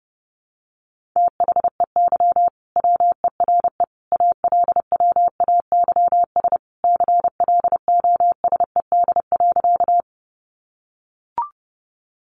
Morse code, heard as a keyed tone, about 30 wpm, 700 Hz. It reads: THEY WERE ALWAYS CLOSED.
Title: Sentences from Top 1400 Words 30wpm